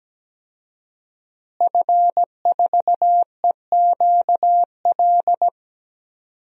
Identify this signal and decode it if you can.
Morse code, heard as a keyed tone, about 17 wpm, 700 Hz. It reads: F4EQL
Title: DXCC-100 Call Signs - 5 Characters - Repeated 3x - 17wpm